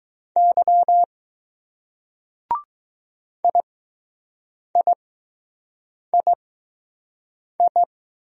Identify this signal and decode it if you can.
Morse code, heard as a keyed tone, about 20 wpm, spaced as for 6 wpm, 700 Hz. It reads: Y IIII